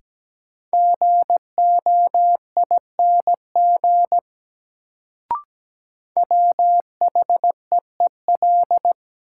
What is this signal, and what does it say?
Morse code, heard as a keyed tone, about 17 wpm, 700 Hz. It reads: GOING WHEEL
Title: Sentences from Top 500 Words - New Word Drill - 17wpm